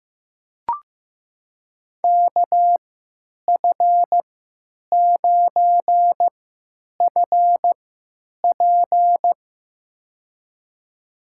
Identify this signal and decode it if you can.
Morse code, heard as a keyed tone, about 15 wpm, spaced as for 9 wpm, 700 Hz. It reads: KF9FP
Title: USA Call Signs - 5 Characters - 3x Character Spacing - 15wpm